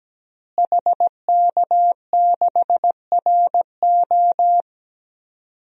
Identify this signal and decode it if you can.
Morse code, heard as a keyed tone, about 17 wpm, 700 Hz. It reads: HK6RO
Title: DXCC-100 Call Signs - 5 Characters - Repeated 3x - 17wpm